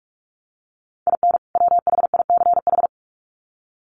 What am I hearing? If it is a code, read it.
Morse code, heard as a keyed tone, about 44 wpm, 700 Hz. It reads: IN WHICH
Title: Sets of 2 Words 45wpm